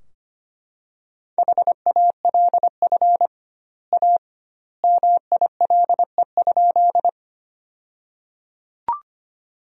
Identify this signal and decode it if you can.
Morse code, heard as a keyed tone, about 25 wpm, 700 Hz. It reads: HALF A MILE?
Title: 2x Word-Spacing - Sentences from Top 400 Words - 25wpm